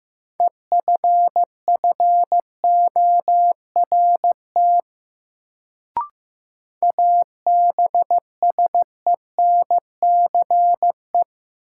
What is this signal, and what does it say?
Morse code, heard as a keyed tone, about 15 wpm, 700 Hz. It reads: EFFORT ABSENCE
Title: Sets of 4 Words - New Word Drill - 15wpm